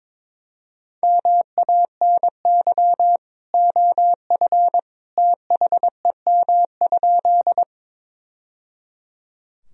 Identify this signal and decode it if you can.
Morse code, heard as a keyed tone, about 22 wpm, 700 Hz. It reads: MANY OF THEM?
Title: Sentences from Top 1000 Words - Review Part 1 - 22wpm